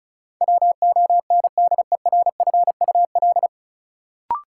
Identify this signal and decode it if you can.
Morse code, heard as a keyed tone, about 36 wpm, 700 Hz. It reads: WONDERFUL
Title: Sentences from Top 1100 Words - New Word Drill Alt - 35wpm